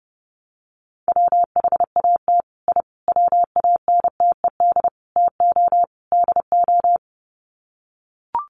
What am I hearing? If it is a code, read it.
Morse code, heard as a keyed tone, about 30 wpm, 700 Hz. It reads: WHAT I WANTED TO DO
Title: Sets of 5 Words Encore2 30wpm